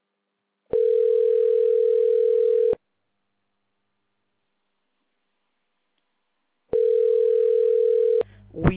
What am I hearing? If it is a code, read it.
Ringback tone (North America), 440 + 480 Hz.